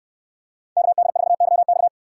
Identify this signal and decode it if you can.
Morse code, heard as a keyed tone, about 64 wpm, 700 Hz. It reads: KD4YX